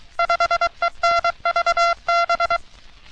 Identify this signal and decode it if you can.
Morse code, heard as a keyed tone, about 23 wpm, 680 Hz. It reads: 5ENVB
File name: U.S. Navy - RN-2849-124 - Sixteens Morse Code Practice